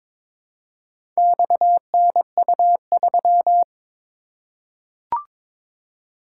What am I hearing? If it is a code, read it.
Morse code, heard as a keyed tone, about 22 wpm, 700 Hz. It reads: XNU3
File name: practice / 22wpm / Four Letter-Number Combinations - 22wpm